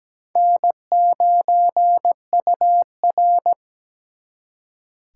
Morse code, heard as a keyed tone, about 20 wpm, 700 Hz. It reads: N9UR
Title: Speed Racing - USA 1x2 Call Signs 17wpm